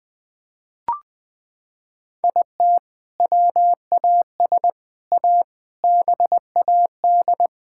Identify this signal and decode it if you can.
Morse code, heard as a keyed tone, about 20 wpm, 700 Hz. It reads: IT WAS A BAD
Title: Sentences from Top 1000 Words - Review Part 6 - 20wpm